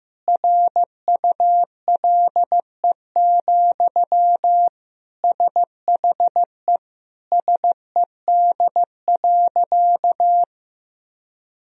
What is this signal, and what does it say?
Morse code, heard as a keyed tone, about 15 wpm, 700 Hz. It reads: RULE, SHE SED.
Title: Sentences from Top 400 Words and Abbreviations 15wpm